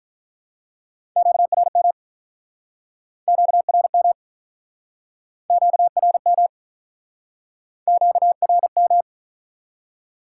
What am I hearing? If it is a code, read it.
Morse code, heard as a keyed tone, about 49 wpm, 700 Hz. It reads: QRM QRM QRM QRM